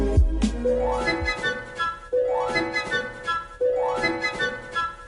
Emergency Alert System (SAME) tones, heard as harsh buzzing bursts. It is heard from 1.2 to 4.6 s.